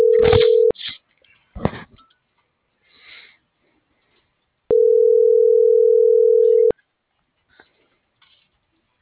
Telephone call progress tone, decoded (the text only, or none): ringback tone